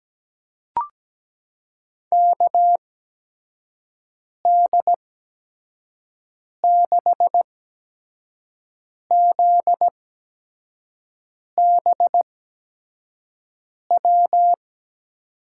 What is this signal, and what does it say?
Morse code, heard as a keyed tone, about 17 wpm, spaced as for 5 wpm, 700 Hz. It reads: KD6ZBW
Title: USA Call Signs - 6 Characters - 8x Character Spacing - 17wpm